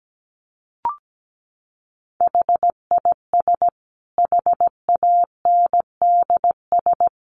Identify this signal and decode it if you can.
Morse code, heard as a keyed tone, about 17 wpm, 700 Hz. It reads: HIS HANDS
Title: A Sound Of Thunder 17wpm